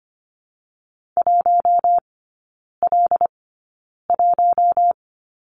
Morse code, heard as a keyed tone, about 25 wpm, spaced as for 9 wpm, 700 Hz. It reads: 1L1
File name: Lesson 14e - Number 2 - Three Characters 25wpm